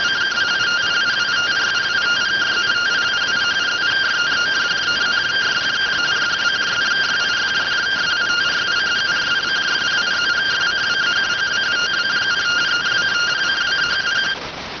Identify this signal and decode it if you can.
Two-tone teleprinter signal, about 170 Hz shift.